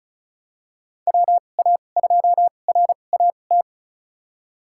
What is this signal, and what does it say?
Morse code, heard as a keyed tone, about 34 wpm, spaced as for 25 wpm, 700 Hz. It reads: WA2RAT
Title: USA Call Signs - 2x Character Spacing - 35wpm